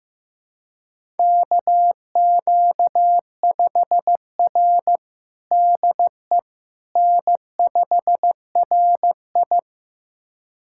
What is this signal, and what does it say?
Morse code, heard as a keyed tone, about 15 wpm, 700 Hz. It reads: KQ5R DE N5RI
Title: USA Callsign DE Callsign - 4 Characters - 15wpm